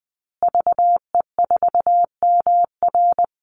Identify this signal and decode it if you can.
Morse code, heard as a keyed tone, about 20 wpm, 700 Hz. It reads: VE4MR
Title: Call Signs Repeated 2x - 20wpm